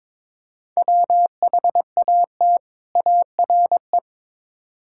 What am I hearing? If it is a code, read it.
Morse code, heard as a keyed tone, about 22 wpm, 700 Hz. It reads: WHAT ARE
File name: practice / 22wpm / Build A Phrase - 5-Word Phrases - 22wpm